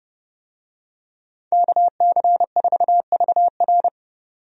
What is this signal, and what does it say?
Morse code, heard as a keyed tone, about 30 wpm, 700 Hz. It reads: KC4VR
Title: USA Call Signs - 5 Characters - 30wpm